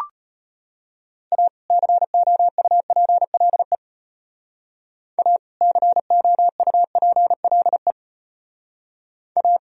Morse code, heard as a keyed tone, about 31 wpm, 700 Hz. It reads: A COUPLE A COUPLE A
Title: Speed-Racing - Sets of 2 Words 25wpm